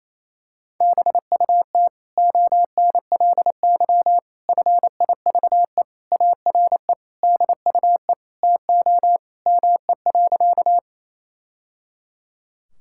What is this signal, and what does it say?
Morse code, heard as a keyed tone, about 28 wpm, 700 Hz. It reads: BUT ONLY FIVE ARE DUE TO ME.
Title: Sentences from Top 1000 Words - Review Part 6 - 28wpm